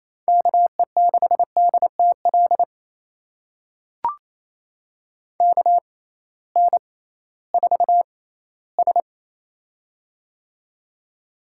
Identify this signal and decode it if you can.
Morse code, heard as a keyed tone, about 28 wpm, spaced as for 16 wpm, 700 Hz. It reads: KE6DTL KN4S
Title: USA Call Signs - 6x Character Spacing - 28wpm